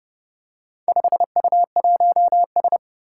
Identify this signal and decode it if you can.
Morse code, heard as a keyed tone, about 30 wpm, 700 Hz. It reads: 5U1S
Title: Four Challenging Letter-Number Combinations 30wpm